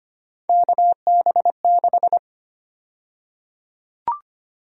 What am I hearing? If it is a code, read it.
Morse code, heard as a keyed tone, about 25 wpm, 700 Hz. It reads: KB6
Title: USA Call Sign Prefixes - Repeated 2x - 25wpm